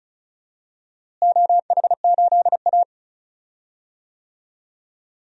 Morse code, heard as a keyed tone, about 34 wpm, 700 Hz. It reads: OH8A
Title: DXCC-100 Call Signs - 4 Characters - Repeated 2x - 35wpm